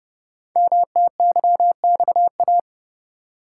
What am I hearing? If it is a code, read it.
Morse code, heard as a keyed tone, about 30 wpm, 700 Hz. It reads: MTYXA
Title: Code Groups - 5 Letters - 30wpm